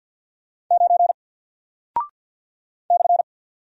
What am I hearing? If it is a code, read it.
Morse code, heard as a keyed tone, about 49 wpm, 700 Hz. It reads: 9 /
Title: Single Character (Letter:Number:Punctuation:Prosign) Alt 50wpm